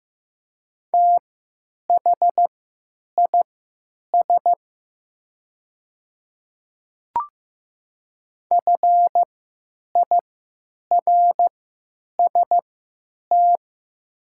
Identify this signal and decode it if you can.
Morse code, heard as a keyed tone, about 15 wpm, spaced as for 9 wpm, 700 Hz. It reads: THIS FIRST
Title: 3x Character Spacing - Top 100 Words - 15wpm